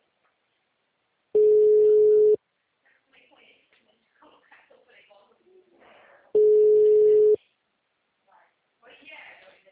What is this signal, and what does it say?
Ringback tone (Europe, 425 Hz), with speech.